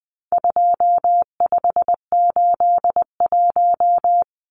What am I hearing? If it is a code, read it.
Morse code, heard as a keyed tone, about 20 wpm, 700 Hz. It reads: 2581
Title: Numbers - 4-Digit 20wpm